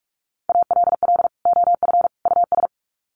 Morse code, heard as a keyed tone, about 44 wpm, 700 Hz. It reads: ALL OF US
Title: Sets of 3 Words 45wpm